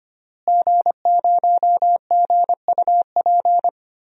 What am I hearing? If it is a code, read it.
Morse code, heard as a keyed tone, about 25 wpm, 700 Hz. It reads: G0GUP
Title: DXCC-100 Call Signs - 5 Characters - Repeated 3x - 25wpm